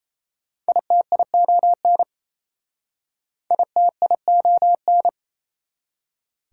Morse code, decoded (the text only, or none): ITION ITION